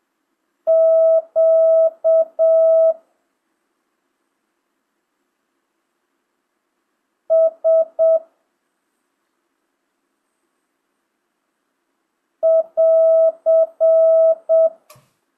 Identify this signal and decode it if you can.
Morse code, heard as a keyed tone, about 7 wpm, 1300 Hz. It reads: Q S <AR>
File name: jh8cbh / 2025cw